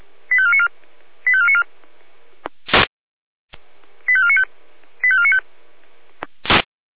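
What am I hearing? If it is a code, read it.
Five-tone selective call (ZVEI), heard as quick runs of five tones, tones of about 70 ms. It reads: 85494 85494 85495 85495
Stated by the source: ZVEI five-tone selective call received on a software-defined radio.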